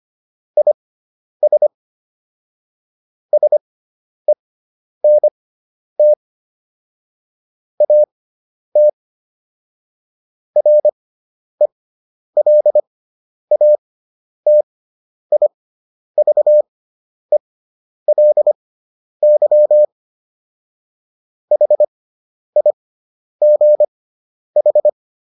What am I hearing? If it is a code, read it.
Morse code, heard as a keyed tone, about 25 wpm, spaced as for 10 wpm, 600 Hz. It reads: IS SENT AT RELATIVELY HIGH